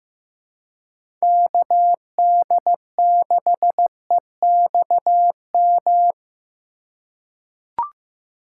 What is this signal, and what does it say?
Morse code, heard as a keyed tone, about 15 wpm, 700 Hz. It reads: KD6EXM
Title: USA Call Signs - 6 Characters - 4x Character Spacing - 15wpm